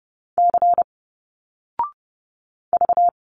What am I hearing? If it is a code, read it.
Morse code, heard as a keyed tone, about 30 wpm, 700 Hz. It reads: C V
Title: Single Letters 30wpm